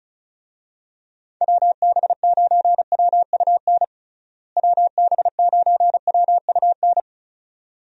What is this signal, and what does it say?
Morse code, heard as a keyed tone, about 36 wpm, 700 Hz. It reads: WB9WUN WB9WUN
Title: USA Call Signs - 6 Characters - Repeated 2x - 35wpm